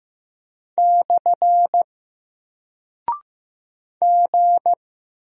Morse code, heard as a keyed tone, about 15 wpm, 700 Hz. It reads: / G
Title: Single Character (Letter:Number:Punctuation:Prosign) 15wpm